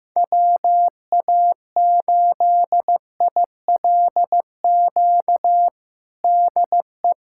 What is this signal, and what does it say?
Morse code, heard as a keyed tone, about 15 wpm, 700 Hz. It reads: WA8ILQ DE
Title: USA Callsign DE Callsign 15wpm